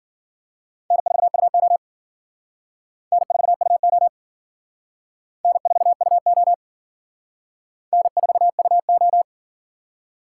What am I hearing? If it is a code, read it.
Morse code, heard as a keyed tone, about 51 wpm, 700 Hz. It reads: N4UO N4UO N4UO N4UO